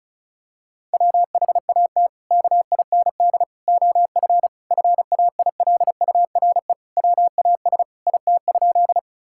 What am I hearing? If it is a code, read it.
Morse code, heard as a keyed tone, about 34 wpm, 700 Hz. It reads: WHAT KIND OF FAILURE WAS IT?
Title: Sentences from Top 1000 Words - Review Part 6 - 35wpm